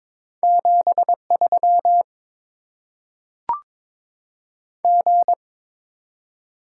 Morse code, heard as a keyed tone, about 22 wpm, 700 Hz. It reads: 73 G
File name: Top 20 QSO Elements - 8x Character Spacing - 22wpm